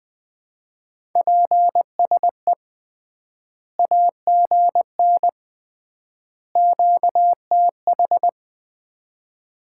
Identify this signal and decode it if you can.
Morse code, heard as a keyed tone, about 20 wpm, 700 Hz. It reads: PSE AGN QTH
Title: Common QSO Phrases - 3x Word-Spacing - 20wpm